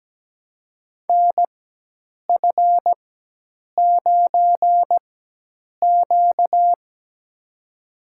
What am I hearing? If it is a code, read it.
Morse code, heard as a keyed tone, about 17 wpm, spaced as for 8 wpm, 700 Hz. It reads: NF9Q
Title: USA Call Signs - 4x Character Spacing - 17wpm